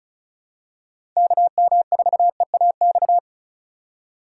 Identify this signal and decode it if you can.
Morse code, heard as a keyed tone, about 36 wpm, 700 Hz. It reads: KM4EAX
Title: USA Call Signs - 6 Characters - 4x Character Spacing - 35wpm